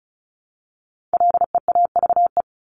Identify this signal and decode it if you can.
Morse code, heard as a keyed tone, about 34 wpm, 700 Hz. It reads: LEAVE
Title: Top 500 Words Repeated 35wpm